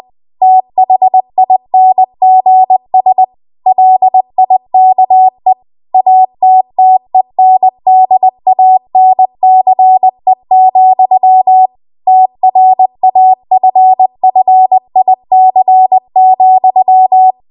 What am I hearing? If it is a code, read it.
Morse code, heard as a keyed tone, about 20 wpm, 750 Hz. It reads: THINGS LIKE ATTENDANCE, TRAFFIC,